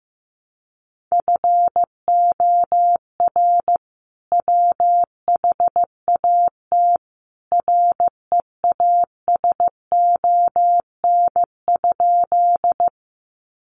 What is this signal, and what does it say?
Morse code, heard as a keyed tone, about 15 wpm, 700 Hz. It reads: FOR WHAT REASON?
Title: Sentences from Top 700 Words 15wpm